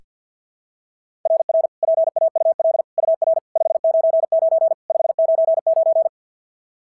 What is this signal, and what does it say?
Morse code, heard as a keyed tone, about 46 wpm, 660 Hz. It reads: RR PAUL UR 599 599
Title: SOTA QSO Practice 50wpm